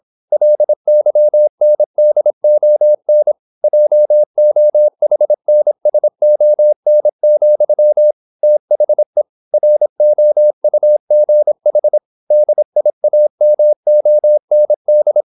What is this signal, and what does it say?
Morse code, heard as a keyed tone, about 26 wpm, 600 Hz. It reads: LYNDON JOHNSON, THE ROUGH DIAMOND
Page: Never 26wpm | CW med Gnister